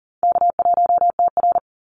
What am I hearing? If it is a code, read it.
Morse code, heard as a keyed tone, about 40 wpm, 700 Hz. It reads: K1TR
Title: Easy Call Signs - 40wpm